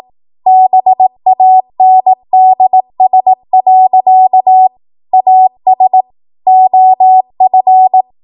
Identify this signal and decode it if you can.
Morse code, heard as a keyed tone, about 18 wpm, 750 Hz. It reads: BANDS. AS OF
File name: Archive / 18 WPM / 18WPM